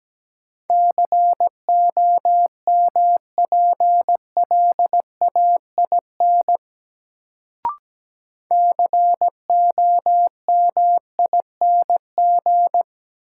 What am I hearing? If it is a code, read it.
Morse code, heard as a keyed tone, about 17 wpm, 700 Hz. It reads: COMPLAIN COMING
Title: Sentences from Top 1300 Words - New Word Drill - 17wpm